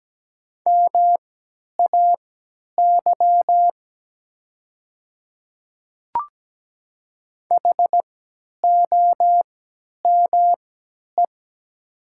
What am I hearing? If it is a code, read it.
Morse code, heard as a keyed tone, about 17 wpm, spaced as for 10 wpm, 690 Hz. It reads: MAY HOME
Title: 3x Character Spacing - Top 200 Words - 17wpm